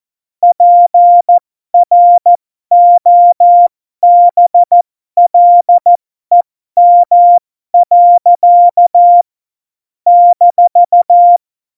Morse code, heard as a keyed tone, about 14 wpm, spaced as for 12 wpm, 700 Hz. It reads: PROBLEM. -
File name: Morsekurs / Witze